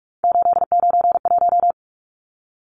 Morse code, heard as a keyed tone, about 44 wpm, 700 Hz. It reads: 891
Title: Numbers - 3-Digit 45wpm